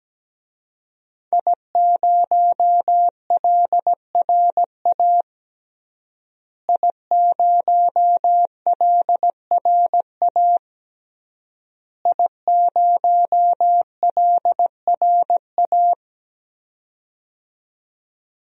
Morse code, heard as a keyed tone, about 17 wpm, 700 Hz. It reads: I0LRA I0LRA I0LRA
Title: DXCC-100 Call Signs - 5 Characters - Repeated 3x - 17wpm